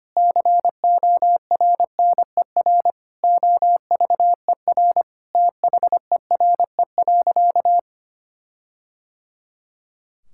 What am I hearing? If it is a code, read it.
Morse code, heard as a keyed tone, about 25 wpm, 700 Hz. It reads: CORNER OVER THERE.
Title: Sentences from Top 1600 Words 25wpm